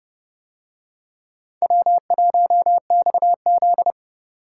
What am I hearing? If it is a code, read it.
Morse code, heard as a keyed tone, about 29 wpm, 700 Hz. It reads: W1XZ